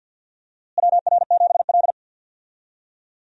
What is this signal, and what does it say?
Morse code, heard as a keyed tone, about 52 wpm, 700 Hz. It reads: WR7L
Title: USA Call Signs - 4 Characters - Repeated 3x - 50wpm